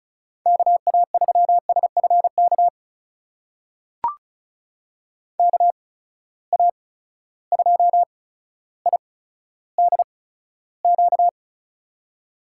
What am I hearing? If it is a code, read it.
Morse code, heard as a keyed tone, about 36 wpm, spaced as for 15 wpm, 700 Hz. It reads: KA3SFK KA2IDQ
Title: USA Call Signs - 6 Characters - 8x Character Spacing - 35wpm